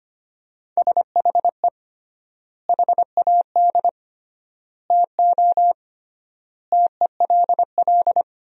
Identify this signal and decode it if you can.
Morse code, heard as a keyed tone, about 25 wpm, 700 Hz. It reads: SHE HAD TO TELL
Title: 3x Word-Spacing - Sentences from Top 200 Words - 25wpm